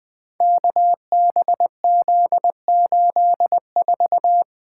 Morse code, heard as a keyed tone, about 20 wpm, 700 Hz. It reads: KBZ84
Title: Code Groups - 5 Letter:Numbers - 20wpm